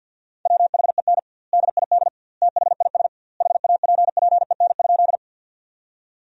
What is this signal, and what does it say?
Morse code, heard as a keyed tone, about 50 wpm, 700 Hz. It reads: WHEN DID THIS HAPPEN?
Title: 2x Word-Spacing - Sentences from Top 400 Words - 50wpm